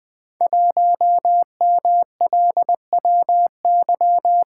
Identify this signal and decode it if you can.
Morse code, heard as a keyed tone, about 20 wpm, 700 Hz. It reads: 1MLWY
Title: Code Groups - 5 Letter:Numbers - 20wpm